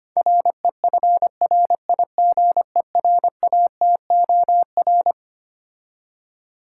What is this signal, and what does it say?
Morse code, heard as a keyed tone, about 25 wpm, 700 Hz. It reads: REFRIGERATOR
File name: practice / 25wpm / Difficult Words - Speed-Racing with Hints - 25wpm